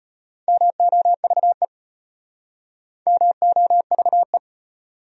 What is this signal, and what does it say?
Morse code, heard as a keyed tone, about 36 wpm, 700 Hz. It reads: MOVE MOVE